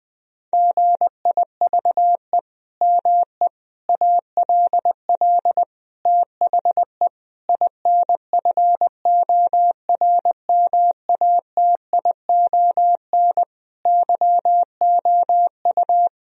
Morse code, heard as a keyed tone, about 20 wpm, 700 Hz. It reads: GIVE ME ALL THE INFORMATION YOU
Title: Sentences from Top 1000 Words - Review Part 3 - 20wpm